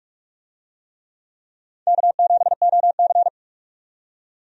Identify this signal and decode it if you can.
Morse code, heard as a keyed tone, about 46 wpm, 700 Hz. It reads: K7OC
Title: USA Call Signs - 4 Characters - 45wpm